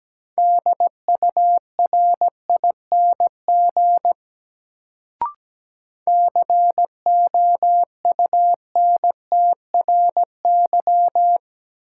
Morse code, heard as a keyed tone, about 17 wpm, 700 Hz. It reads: DURING COUNTRY
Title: Sets of 2 Words - New Word Drill - 17wpm